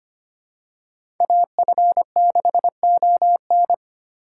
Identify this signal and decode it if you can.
Morse code, heard as a keyed tone, about 25 wpm, 700 Hz. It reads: AF6ON
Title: USA Call Signs - 2x Character Spacing - 25wpm